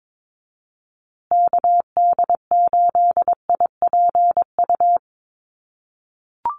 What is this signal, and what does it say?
Morse code, heard as a keyed tone, about 22 wpm, 700 Hz. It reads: KD8IPU
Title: USA Call Signs Repeated 2x 22wpm